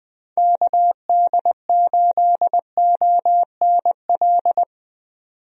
Morse code, heard as a keyed tone, about 21 wpm, 700 Hz. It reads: KD8ONL
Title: USA Call Signs - 6 Characters - 20wpm